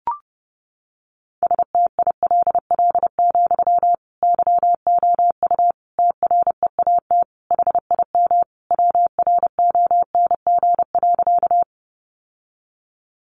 Morse code, heard as a keyed tone, about 30 wpm, 700 Hz. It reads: STILL, YOU TREAT HIM WRONG.
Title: Sentences from Top 1300 Words 30wpm